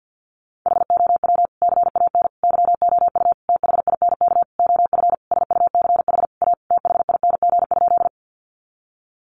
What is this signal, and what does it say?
Morse code, heard as a keyed tone, about 48 wpm, 700 Hz. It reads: HOW CAN YOU THINK OF SUCH A THING?
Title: Sentences from Top 200 Words 50wpm